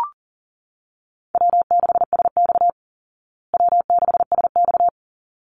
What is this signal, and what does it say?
Morse code, heard as a keyed tone, about 37 wpm, 700 Hz. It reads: W6SX W6SX
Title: Call Signs Repeated 3x - 40wpm